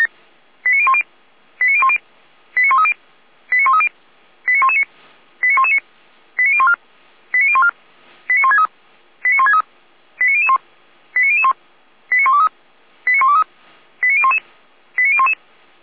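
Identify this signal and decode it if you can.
Five-tone selective call (ZVEI), heard as quick runs of five tones, tones of about 70 ms. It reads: 89010 89010 89130 89130 89119 89119 89014 89014 89163 89163 89001 89001 89123 89123 89011 89011
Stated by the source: ZVEI five-tone selective call received by SDR.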